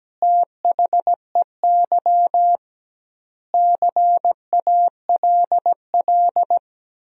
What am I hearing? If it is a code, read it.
Morse code, heard as a keyed tone, about 17 wpm, 700 Hz. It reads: THEY CALL
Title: 2x Word-Spacing - Sentences from Top 100 Words - 17wpm